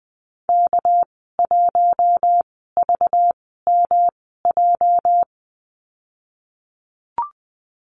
Morse code, heard as a keyed tone, about 20 wpm, spaced as for 14 wpm, 700 Hz. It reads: K1VMJ
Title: 213 Lesson 34g - Number 8 - Call Signs 20wpm